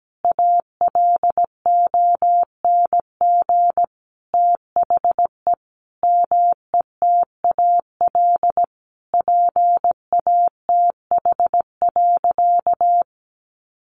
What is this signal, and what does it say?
Morse code, heard as a keyed tone, about 17 wpm, 700 Hz. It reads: ALONG THE METAL PATH.
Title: A Sound Of Thunder 17wpm